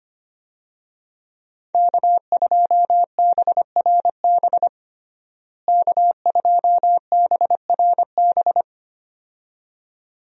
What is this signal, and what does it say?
Morse code, heard as a keyed tone, about 25 wpm, 700 Hz. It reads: K2BRB K2BRB
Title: USA Call Signs - 5 Characters - Repeated 2x - 25wpm